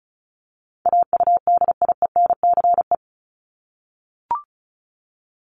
Morse code, heard as a keyed tone, about 34 wpm, 700 Hz. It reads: AUDIENCE
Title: Top 751-1000 Words 35wpm